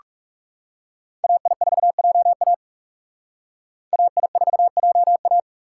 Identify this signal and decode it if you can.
Morse code, heard as a keyed tone, about 38 wpm, 700 Hz. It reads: AI4JA AI4JA